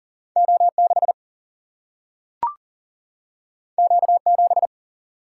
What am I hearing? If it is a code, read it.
Morse code, heard as a keyed tone, about 39 wpm, 700 Hz. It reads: O6 Q7
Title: Single-Letter Single-Number 40wpm